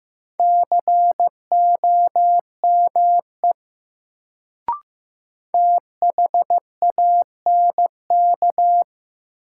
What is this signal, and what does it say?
Morse code, heard as a keyed tone, about 15 wpm, 700 Hz. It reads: COME THANK
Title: Sets of 2 Words - New Word Drill - 15wpm